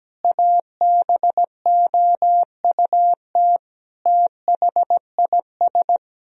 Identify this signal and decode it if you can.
Morse code, heard as a keyed tone, about 17 wpm, 700 Hz. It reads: ABOUT THIS
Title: Sentences from Top 1000 Words - Review Part 1 - 17wpm